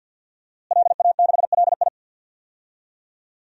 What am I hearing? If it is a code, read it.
Morse code, heard as a keyed tone, about 50 wpm, 700 Hz. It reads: LA6LI
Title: DXCC-100 Call Signs - 5 Characters - Repeated 2x - 50wpm